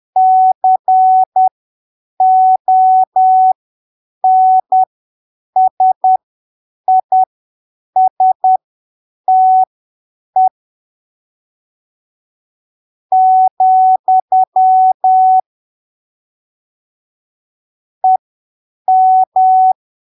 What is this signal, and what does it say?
Morse code, heard as a keyed tone, about 10 wpm, spaced as for 7 wpm, 750 Hz. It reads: CONSISTE , EM
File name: arquivos / texto / texto 11 10ppm